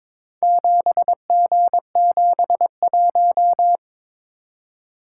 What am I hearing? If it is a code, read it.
Morse code, heard as a keyed tone, about 22 wpm, 700 Hz. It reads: 7G71
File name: practice / 22wpm / Four Letter-Number Combinations - 22wpm